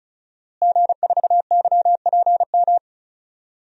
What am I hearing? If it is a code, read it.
Morse code, heard as a keyed tone, about 34 wpm, 700 Hz. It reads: G4YPM
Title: DXCC-100 Call Signs - 5 Characters - Repeated 2x - 35wpm